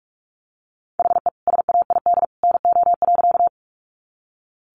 Morse code, heard as a keyed tone, about 44 wpm, 700 Hz. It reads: HE SAID NO.
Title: Sentences from Top 100 Words 45wpm